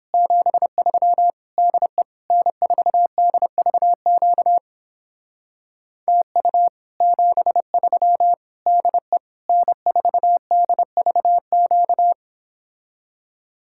Morse code, heard as a keyed tone, about 29 wpm, 700 Hz. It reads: 73 DE N4DVQ TU 73 DE N4DVQ